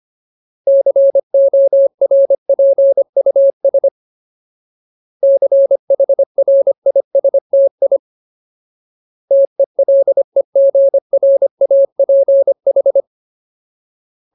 Morse code, heard as a keyed tone, about 25 wpm, 550 Hz. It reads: CORPUS CHRISTI TELEGRAPH
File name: Ultimate Word List / 25WPM Ultimate